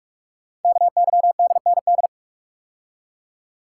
Morse code, heard as a keyed tone, about 44 wpm, 700 Hz. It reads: KYDND
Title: Code Groups - 5 Letters - 45wpm